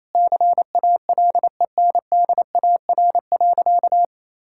Morse code, heard as a keyed tone, about 28 wpm, 700 Hz. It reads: CALENDAR.
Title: Sentences from Top 1600 Words 28wpm